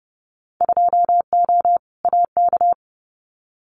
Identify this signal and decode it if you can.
Morse code, heard as a keyed tone, about 29 wpm, 700 Hz. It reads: 2O AK
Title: Winter Field Day Exchanges Encore 30wpm